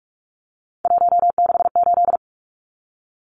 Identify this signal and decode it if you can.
Morse code, heard as a keyed tone, about 44 wpm, 700 Hz. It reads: 168
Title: Numbers - 3-Digit 45wpm